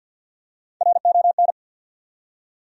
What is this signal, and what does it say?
Morse code, heard as a keyed tone, about 53 wpm, 700 Hz. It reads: RON
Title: USA Common Names - Max 3 Letters - 50wpm